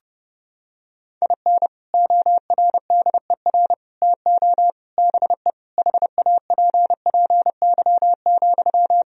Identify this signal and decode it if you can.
Morse code, heard as a keyed tone, about 30 wpm, 700 Hz. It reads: IN ORDER TO BE HAPPY,
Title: Sentences from Top 1000 Words - Review Part 2 - 30wpm